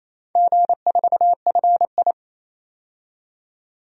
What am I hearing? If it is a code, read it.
Morse code, heard as a keyed tone, about 28 wpm, 700 Hz. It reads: G4FI
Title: DXCC-100 Call Signs - 4 Characters - 28wpm